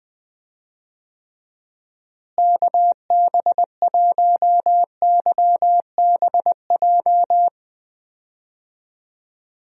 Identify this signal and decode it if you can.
Morse code, heard as a keyed tone, about 21 wpm, 700 Hz. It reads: KB1YBJ
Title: USA Call Signs - 6 Characters - Repeated 2x - 20wpm